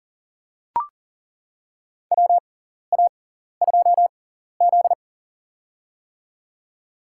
Morse code, heard as a keyed tone, about 39 wpm, spaced as for 14 wpm, 700 Hz. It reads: WA2Z
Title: USA Call Signs - 6x Character Spacing - 40wpm